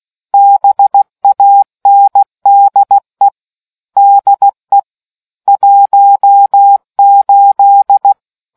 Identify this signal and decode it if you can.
Morse code, heard as a keyed tone, about 16 wpm, 800 Hz. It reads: BANDE DE 18